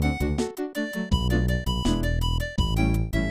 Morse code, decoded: none